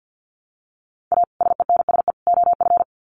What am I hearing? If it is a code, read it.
Morse code, heard as a keyed tone, about 50 wpm, 700 Hz. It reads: A SENSE OF
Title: Sets of 3 Words 50wpm